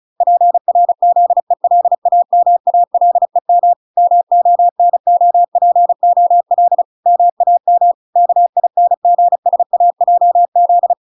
Morse code, heard as a keyed tone, about 36 wpm, 700 Hz. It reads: PRZELAMALEM MONOPOL MAM KINGSAJZ